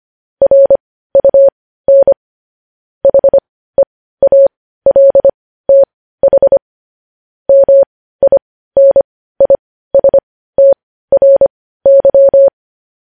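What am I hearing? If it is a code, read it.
Morse code, heard as a keyed tone, about 25 wpm, spaced as for 15 wpm, 550 Hz. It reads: RUN HEALTH MINISTRY
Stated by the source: Morse news headlines practice at 15 wpm.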